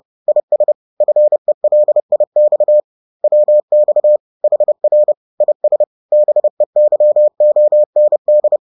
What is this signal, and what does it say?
Morse code, heard as a keyed tone, about 30 wpm, 600 Hz. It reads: IS FELIX WX HR IS BEYOND